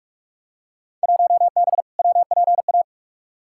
Morse code, heard as a keyed tone, about 42 wpm, 700 Hz. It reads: 1B WPA
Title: Field Day Exchanges - 45wpm